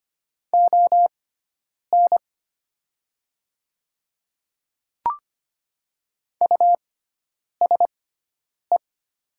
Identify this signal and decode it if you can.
Morse code, heard as a keyed tone, about 25 wpm, spaced as for 9 wpm, 700 Hz. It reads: ON USE
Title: 6x Character Spacing - Top 100 Words - 25wpm